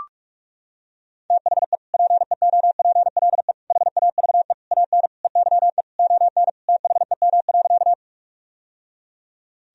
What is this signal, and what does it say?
Morse code, heard as a keyed tone, about 41 wpm, 700 Hz. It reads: THE PEOPLE HAVE AN EYE ON THEM.
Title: Sentences from Top 1000 Words - Review Part 1 - 45wpm